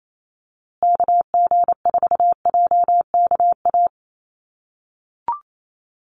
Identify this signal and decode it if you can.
Morse code, heard as a keyed tone, about 28 wpm, 700 Hz. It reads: KG4JKA
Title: USA Call Signs Repeated 2x 28wpm